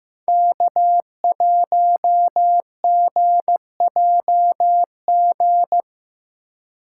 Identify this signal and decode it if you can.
Morse code, heard as a keyed tone, about 15 wpm, 700 Hz. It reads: K1GJG
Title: USA Call Signs - 5 Characters - 2x Character Spacing - 15wpm